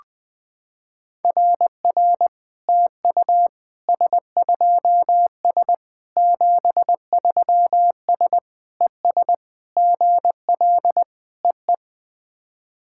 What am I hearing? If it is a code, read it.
Morse code, heard as a keyed tone, about 20 wpm, 700 Hz. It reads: RR TU S2S 73S ES GL EE
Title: SOTA QSO Practice 20wpm